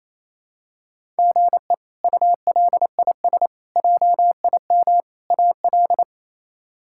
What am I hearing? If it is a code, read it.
Morse code, heard as a keyed tone, about 28 wpm, 700 Hz. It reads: GE ULIS JIM AL
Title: K1USN SST QSO Practice - 28wpm